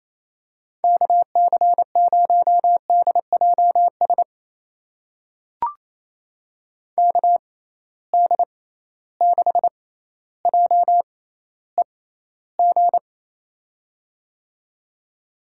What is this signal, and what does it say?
Morse code, heard as a keyed tone, about 28 wpm, spaced as for 14 wpm, 700 Hz. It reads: KC0DJS KD6JEG
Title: USA Call Signs - 6 Characters - 6x Character Spacing - 28wpm